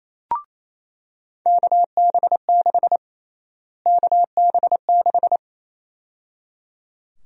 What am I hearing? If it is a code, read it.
Morse code, heard as a keyed tone, about 28 wpm, 700 Hz. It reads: KB6 KB6
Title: USA Call Sign Prefixes - Repeated 2x - 28wpm